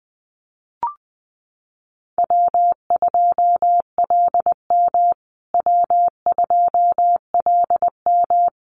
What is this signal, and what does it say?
Morse code, heard as a keyed tone, about 20 wpm, 700 Hz. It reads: W2LM W2LM
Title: USA Call Signs Repeated 3x 20wpm